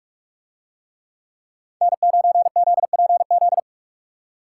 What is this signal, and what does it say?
Morse code, heard as a keyed tone, about 43 wpm, 700 Hz. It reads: N9ZPZ